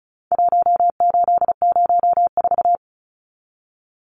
Morse code, heard as a keyed tone, about 34 wpm, 700 Hz. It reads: 1804
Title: Numbers - Contest Style - 4-Digit 35wpm